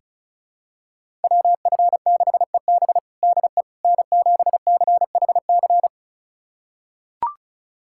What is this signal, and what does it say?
Morse code, heard as a keyed tone, about 34 wpm, 700 Hz. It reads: WF6EB DE N7CHC
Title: USA Callsign DE Callsign - 5 Characters - 35wpm